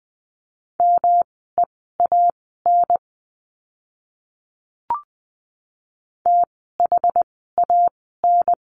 Morse code, heard as a keyed tone, about 20 wpm, spaced as for 14 wpm, 700 Hz. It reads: MEAN THAN
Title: 230 Lesson 37f - Letter x - Words 20wpm